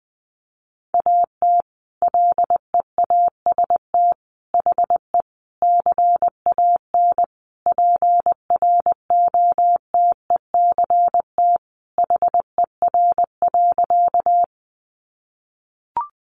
Morse code, heard as a keyed tone, about 20 wpm, 700 Hz. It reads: AT LEAST HE CAN PROTECT HER.
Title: Sentences from Top 900 Words 20wpm